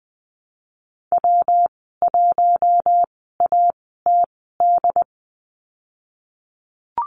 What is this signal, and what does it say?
Morse code, heard as a keyed tone, about 20 wpm, spaced as for 15 wpm, 700 Hz. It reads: W1ATD
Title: 167 Lesson 27g - Number 7 - Call Signs 20wpm